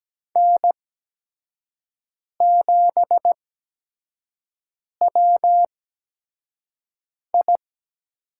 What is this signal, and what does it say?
Morse code, heard as a keyed tone, about 17 wpm, spaced as for 5 wpm, 700 Hz. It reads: N7WI
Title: USA Call Signs - 5 Characters - 8x Character Spacing - 17wpm